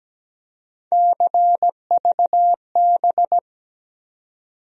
Morse code, heard as a keyed tone, about 17 wpm, 700 Hz. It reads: CVB
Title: USA Call Sign Postfixes - 3 Letters 17wpm